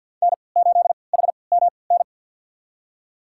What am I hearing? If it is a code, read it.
Morse code, heard as a keyed tone, about 51 wpm, spaced as for 28 wpm, 700 Hz. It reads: N8HMN
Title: USA Call Signs - 5 Characters - 3x Character Spacing - 50wpm